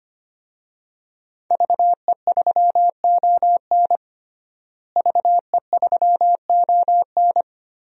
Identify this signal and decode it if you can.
Morse code, heard as a keyed tone, about 25 wpm, 700 Hz. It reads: VE3ON VE3ON
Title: DXCC-100 Call Signs - 5 Characters - Repeated 2x - 25wpm